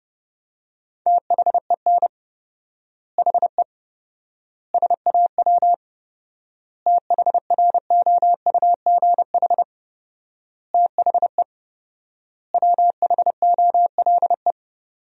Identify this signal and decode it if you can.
Morse code, heard as a keyed tone, about 30 wpm, 700 Hz. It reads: THEN HE SAW THROUGH THE WHOLE
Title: 4x Word-Spacing - Sentences from Top 400 Words - 30wpm